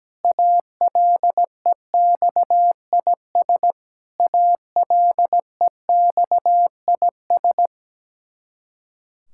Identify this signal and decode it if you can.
Morse code, heard as a keyed tone, about 17 wpm, 700 Hz. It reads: ALEXIS ALEXIS
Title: Practice QSO - Exchange 1 - Beginning - 17wpm